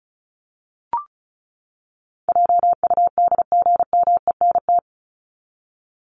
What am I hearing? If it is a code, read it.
Morse code, heard as a keyed tone, about 34 wpm, 700 Hz. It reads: JUDGMENT
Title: Top 1751-2000 Words 35wpm